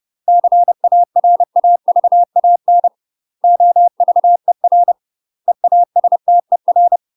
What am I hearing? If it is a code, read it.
Morse code, heard as a keyed tone, about 30 wpm, 700 Hz. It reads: CARAVAN OVER EASTER
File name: UK MASTER QSOs TRAINING / 30WPM / UK MASTER QSOs 0043